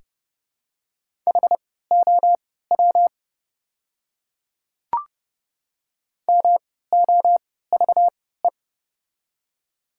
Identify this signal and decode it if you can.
Morse code, heard as a keyed tone, about 31 wpm, spaced as for 17 wpm, 700 Hz. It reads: HOW MOVE
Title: 3x Character Spacing - Top 300 Words - 30wpm